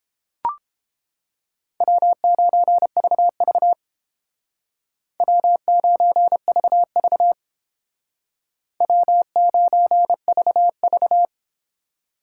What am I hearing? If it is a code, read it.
Morse code, heard as a keyed tone, about 30 wpm, 700 Hz. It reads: W9VV W9VV W9VV